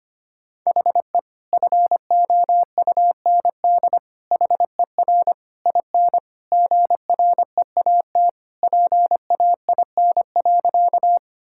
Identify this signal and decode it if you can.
Morse code, heard as a keyed tone, about 25 wpm, 700 Hz. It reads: HE FOUND HER IN GREAT PAIN.
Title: Sentences from Top 1600 Words 25wpm